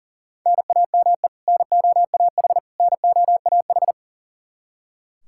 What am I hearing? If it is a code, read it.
Morse code, heard as a keyed tone, about 37 wpm, 700 Hz. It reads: NAME NOAH NOAH
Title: Practice QSO - Exchange 1 - Beginning - 40wpm